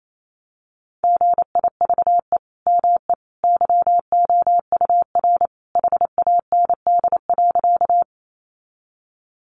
Morse code, heard as a keyed tone, about 28 wpm, 700 Hz. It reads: GIVE ME YOUR HAND.
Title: Sentences from Top 200 Words 28wpm